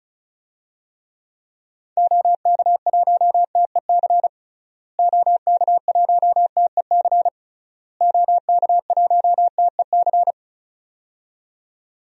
Morse code, heard as a keyed tone, about 36 wpm, 700 Hz. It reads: OK1TEC OK1TEC OK1TEC
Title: DXCC-100 Call Signs - 6 Characters - Repeated 3x - 35wpm